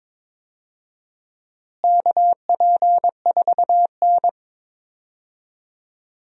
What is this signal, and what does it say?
Morse code, heard as a keyed tone, about 22 wpm, 700 Hz. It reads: KP4N
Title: DXCC-100 Call Signs - 4 Characters - Repeated 3x - 22wpm